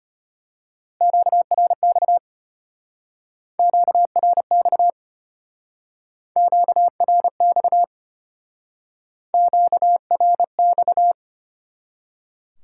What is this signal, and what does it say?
Morse code, heard as a keyed tone, about 32 wpm, 700 Hz. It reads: QRX QRX QRX QRX